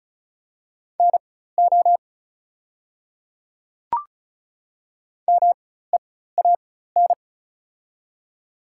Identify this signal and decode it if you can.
Morse code, heard as a keyed tone, about 35 wpm, spaced as for 16 wpm, 690 Hz. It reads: NO MEAN
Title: 4x Character Spacing - Top 300 Words - 35wpm